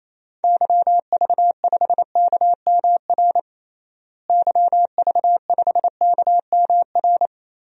Morse code, heard as a keyed tone, about 28 wpm, 700 Hz. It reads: YV5KMR YV5KMR
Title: DXCC-100 Call Signs - 6 Characters - Repeated 3x - 28wpm